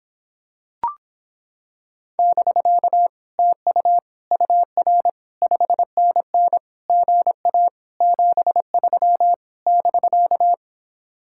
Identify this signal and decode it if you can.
Morse code, heard as a keyed tone, about 26 wpm, 700 Hz. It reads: <BK> TU UR 5NN GA 73 <BK>